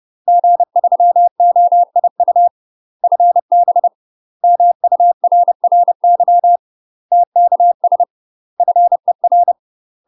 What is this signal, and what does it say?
Morse code, heard as a keyed tone, about 30 wpm, 700 Hz. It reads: G3OIU FB MURRY TKS FER
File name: UK MASTER QSOs TRAINING / 30WPM / UK MASTER QSOs 0029